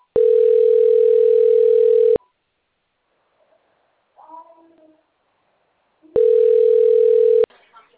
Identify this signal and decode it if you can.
Ringback tone (North America), 440 + 480 Hz.
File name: rg-1600-unknown- - - .76311